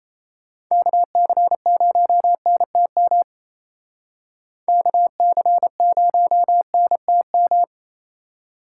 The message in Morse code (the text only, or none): KC0NTM KC0NTM